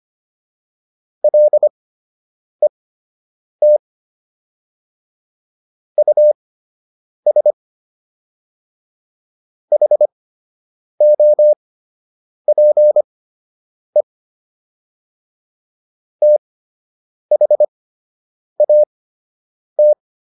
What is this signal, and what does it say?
Morse code, heard as a keyed tone, about 25 wpm, spaced as for 8 wpm, 600 Hz. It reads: LET US HOPE THAT